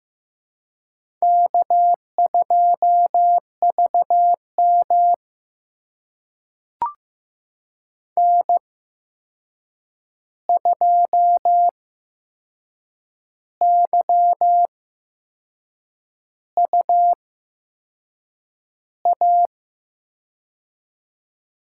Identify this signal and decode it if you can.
Morse code, heard as a keyed tone, about 15 wpm, spaced as for 6 wpm, 700 Hz. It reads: K2VM N2YUA
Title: USA Call Signs - 8x Character Spacing - 15wpm